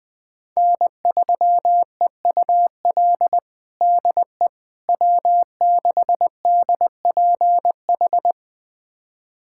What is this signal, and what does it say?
Morse code, heard as a keyed tone, about 21 wpm, 700 Hz. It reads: N3EUL DE W6DPH
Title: USA Callsign DE Callsign - 5 Characters - 20wpm